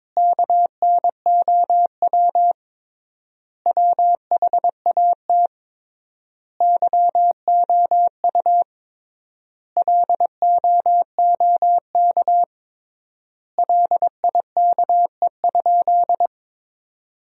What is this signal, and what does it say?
Morse code, heard as a keyed tone, about 22 wpm, 700 Hz. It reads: KNOW WHAT YOU LOOK LIKE?
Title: 3x Word-Spacing - Sentences from Top 100 Words - 22wpm